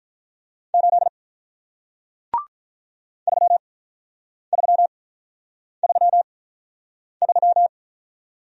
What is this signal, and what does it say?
Morse code, heard as a keyed tone, about 47 wpm, spaced as for 9 wpm, 700 Hz. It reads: 8 3333